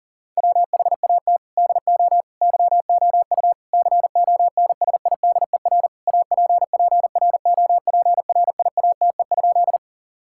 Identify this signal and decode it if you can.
Morse code, heard as a keyed tone, about 40 wpm, 690 Hz. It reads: WHAT DO YOU CONSIDER APPROPRIATE?
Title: Sentences from Top 1000 Words - Review Part 5 - 40wpm